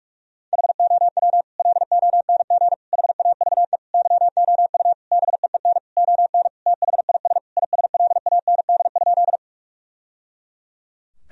Morse code, heard as a keyed tone, about 41 wpm, 700 Hz. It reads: HOW LONG HAVE YOU BEEN ON THIS ISLAND?
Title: Sentences from Top 1000 Words - Review Part 3 - 45wpm